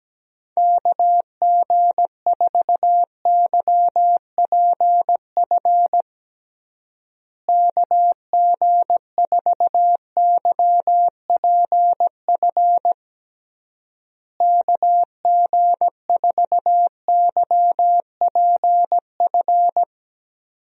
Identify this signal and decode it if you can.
Morse code, heard as a keyed tone, about 17 wpm, 700 Hz. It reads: KG4YPF KG4YPF KG4YPF
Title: USA Call Signs - 6 Characters - Repeated 3x - 17wpm